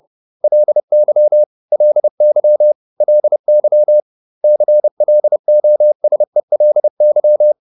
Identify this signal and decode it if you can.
Morse code, heard as a keyed tone, about 30 wpm, 600 Hz. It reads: LY LY LY CLOSELY